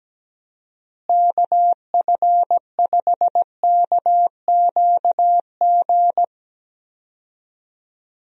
Morse code, heard as a keyed tone, about 17 wpm, 700 Hz. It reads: KF5KQG
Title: USA Call Signs - 6 Characters - Repeated 2x - 17wpm